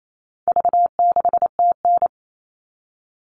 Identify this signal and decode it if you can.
Morse code, heard as a keyed tone, about 28 wpm, 700 Hz. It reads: V6TN
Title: All Cut Numbers - 4-Digit 28wpm